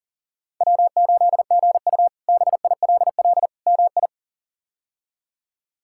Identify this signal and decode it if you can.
Morse code, heard as a keyed tone, about 40 wpm, 700 Hz. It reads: W8GU BILL MI
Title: K1USN SST QSO Practice - 40wpm